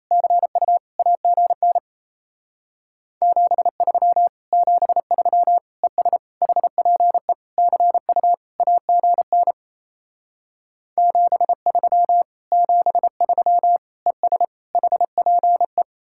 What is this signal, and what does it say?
Morse code, heard as a keyed tone, about 36 wpm, 700 Hz. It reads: CU AGN 73 73 ES HPE CU AGN 73 73 ES HPE